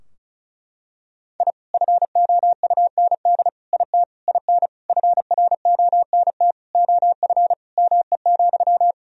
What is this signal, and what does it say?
Morse code, heard as a keyed tone, about 36 wpm, 700 Hz. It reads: I FOUND IT IN FRONT OF ME,
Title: Sentences from Top 1000 Words - Review Part 2 - 35wpm